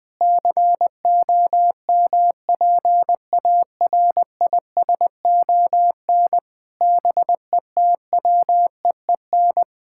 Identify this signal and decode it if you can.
Morse code, heard as a keyed tone, about 20 wpm, 700 Hz. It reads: COMPARISON BETWEEN
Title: Sentences from Top 1600 Words 20wpm